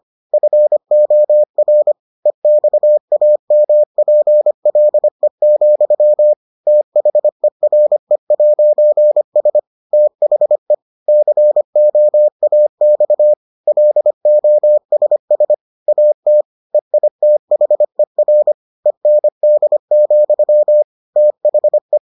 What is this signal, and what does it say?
Morse code, heard as a keyed tone, about 25 wpm, 600 Hz. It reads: FOR EXAMPLE, THERE'S THE COAX LOSS AT EITHER END, THE